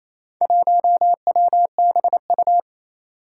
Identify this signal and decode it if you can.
Morse code, heard as a keyed tone, about 27 wpm, 700 Hz. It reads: 1WBU
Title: Four Challenging Letter-Number Combinations 28wpm